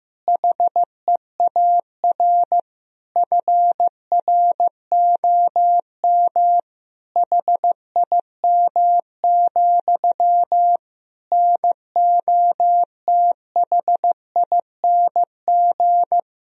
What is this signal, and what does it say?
Morse code, heard as a keyed tone, about 15 wpm, 700 Hz. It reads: HEAR FROM HIM, NOTHING
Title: Sentences from Top 1000 Words - Review Part 1 - 15wpm